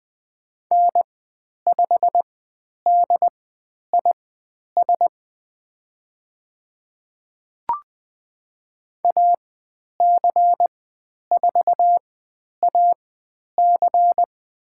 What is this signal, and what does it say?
Morse code, heard as a keyed tone, about 20 wpm, spaced as for 10 wpm, 700 Hz. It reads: N5DIS AC4AC
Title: 109 Lesson 18g - Letter m - Call Signs 20wpm